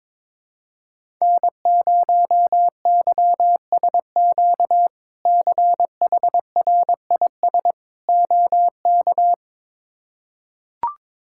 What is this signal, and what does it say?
Morse code, heard as a keyed tone, about 22 wpm, 700 Hz. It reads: N0YSQ CHRIS OK
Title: K1USN SST QSO Practice - 22wpm